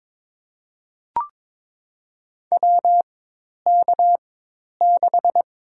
Morse code, heard as a keyed tone, about 22 wpm, spaced as for 10 wpm, 700 Hz. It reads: WK6
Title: USA Call Signs - 4x Character Spacing - 22wpm